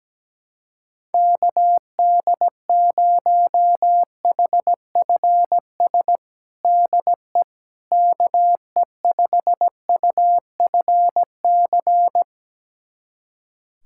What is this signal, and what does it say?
Morse code, heard as a keyed tone, about 17 wpm, 700 Hz. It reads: KD0HFS DE KE5UFC
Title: USA Callsign DE Callsign - 6 Characters - 17wpm